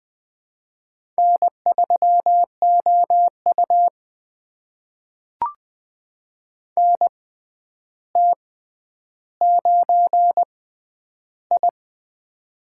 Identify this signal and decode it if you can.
Morse code, heard as a keyed tone, about 20 wpm, spaced as for 10 wpm, 700 Hz. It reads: N3OU NT9I
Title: USA Call Signs - 4 Characters - 6x Character Spacing - 20wpm